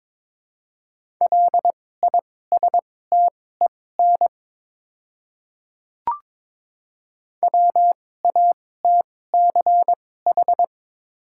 Morse code, heard as a keyed tone, about 22 wpm, spaced as for 16 wpm, 700 Hz. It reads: LISTEN WATCH
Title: 2x Character Spacing - Top 500 Words - 22wpm